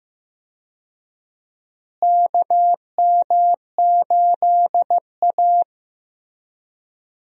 Morse code, heard as a keyed tone, about 15 wpm, 700 Hz. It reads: KM8A
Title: USA Call Signs - 4 Characters - Repeated 3x - 15wpm